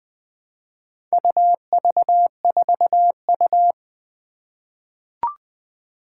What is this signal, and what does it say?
Morse code, heard as a keyed tone, about 20 wpm, 700 Hz. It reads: UV4U
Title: Four Challenging Letter-Number Combinations 20wpm